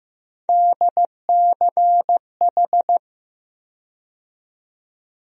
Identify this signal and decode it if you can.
Morse code, heard as a keyed tone, about 15 wpm, 700 Hz. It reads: DCH
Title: USA Call Sign Postfixes - 3 Letters 15wpm